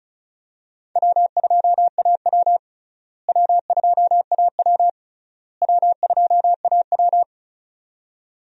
Morse code, heard as a keyed tone, about 36 wpm, 700 Hz. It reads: W2AW W2AW W2AW
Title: USA Call Signs - 4 Characters - Repeated 3x - 35wpm